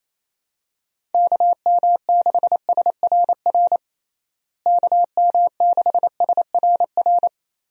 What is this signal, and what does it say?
Morse code, heard as a keyed tone, about 28 wpm, 700 Hz. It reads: KM6SRR KM6SRR
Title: USA Call Signs - 6 Characters - Repeated 3x - 28wpm